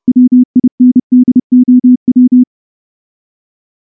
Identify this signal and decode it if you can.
Morse code, heard as a keyed tone, about 28 wpm, 250 Hz. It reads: WINDOW
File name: morse-training-30WPM